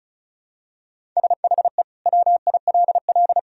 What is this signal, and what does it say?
Morse code, heard as a keyed tone, about 34 wpm, 700 Hz. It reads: SHE WILL
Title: Sentences from Top 1000 Words - Review Part 1 - 35wpm